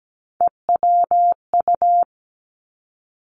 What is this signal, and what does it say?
Morse code, heard as a keyed tone, about 17 wpm, 700 Hz. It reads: EWU
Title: Three Challenging Letter-Number Combinations 17wpm